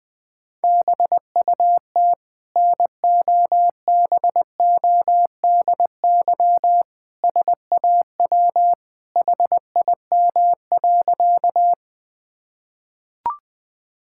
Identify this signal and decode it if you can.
Morse code, heard as a keyed tone, about 20 wpm, 700 Hz. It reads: BUT NOBODY SAW HIM.
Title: Sentences from Top 1600 Words 20wpm